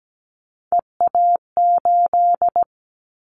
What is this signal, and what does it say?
Morse code, heard as a keyed tone, about 17 wpm, 700 Hz. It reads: EA8
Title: Callsign Prefixes for Top 100 DXCC - 17wpm